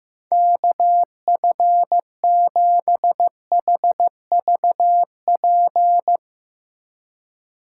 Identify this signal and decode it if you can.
Morse code, heard as a keyed tone, about 16 wpm, 700 Hz. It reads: KF7HVP
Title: Speed Racing - USA 2x3 Call Signs 15wpm